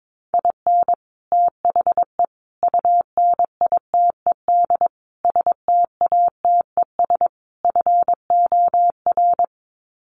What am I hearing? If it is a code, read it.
Morse code, heard as a keyed tone, about 22 wpm, 700 Hz. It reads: IN THE UNITED STATES FOR
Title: Sets of 5 Words Encore2 22wpm